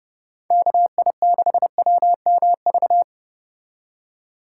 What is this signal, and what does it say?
Morse code, heard as a keyed tone, about 29 wpm, 700 Hz. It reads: KI6WMV